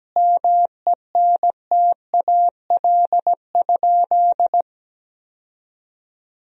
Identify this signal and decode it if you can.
Morse code, heard as a keyed tone, about 17 wpm, 700 Hz. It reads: MENTAL?
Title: Sentences from Top 1000 Words - Review Part 4 - 17wpm